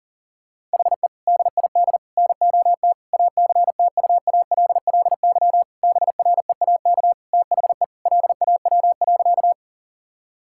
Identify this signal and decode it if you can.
Morse code, heard as a keyed tone, about 40 wpm, 700 Hz. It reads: HE DID NOT ACTUALLY BREAK THE LAW.
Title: Sentences from Top 1000 Words - Review Part 6 - 40wpm